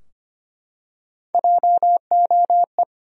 Morse code, heard as a keyed tone, about 25 wpm, 700 Hz. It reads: JOE
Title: USA Common Names - Max 5 Letters - 25wpm